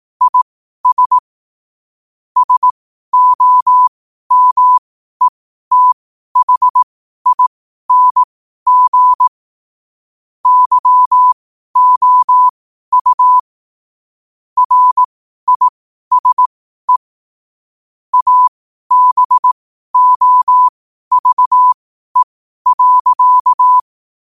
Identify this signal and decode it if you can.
Morse code, heard as a keyed tone, about 18 wpm, spaced as for 12 wpm, 1000 Hz. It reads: IS SOMETHING YOU RISE ABOVE.
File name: qotdfiles / 12wpm